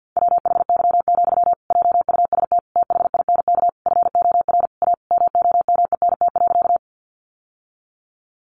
Morse code, heard as a keyed tone, about 48 wpm, 700 Hz. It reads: WHY, JUST THINK FOR A MOMENT.
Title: Sentences from Top 1000 Words 50wpm